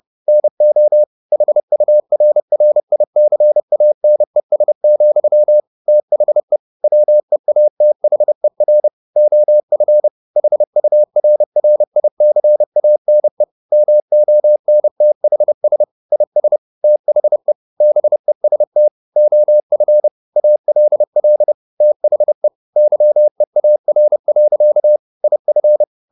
Morse code, heard as a keyed tone, about 30 wpm, 600 Hz. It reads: NO HURRICANES, THE WEATHER OF HURRICANE MONTHS IS THE BEST OF ALL THE YEAR. IF
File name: the old man and the sea / the old man and the sea 1 30wpm